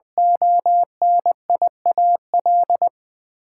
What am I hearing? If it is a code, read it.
Morse code, heard as a keyed tone, about 20 wpm, 700 Hz. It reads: ONIAL
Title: Build A Word - 2 Letters At A Time - 8-Letter Words - 20wpm